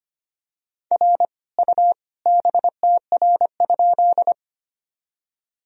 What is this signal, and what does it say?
Morse code, heard as a keyed tone, about 25 wpm, 700 Hz. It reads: R U BTR?
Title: Sentences from Top 400 Words and Abbreviations 25wpm